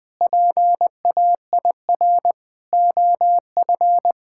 Morse code, heard as a keyed tone, about 20 wpm, 700 Hz. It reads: PAIR OF
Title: Sentences with 3-Word Phrases - Spoken Beginning - 20wpm